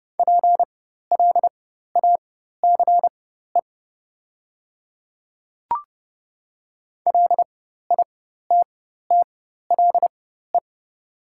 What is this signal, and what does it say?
Morse code, heard as a keyed tone, about 30 wpm, spaced as for 14 wpm, 690 Hz. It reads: PLACE LITTLE
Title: 4x Character Spacing - Top 200 Words - 30wpm